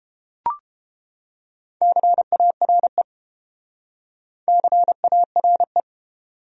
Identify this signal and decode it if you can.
Morse code, heard as a keyed tone, about 31 wpm, 700 Hz. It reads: CARE CARE